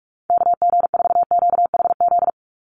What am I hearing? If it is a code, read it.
Morse code, heard as a keyed tone, about 44 wpm, 700 Hz. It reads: KG4QHZ
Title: USA Call Signs 45wpm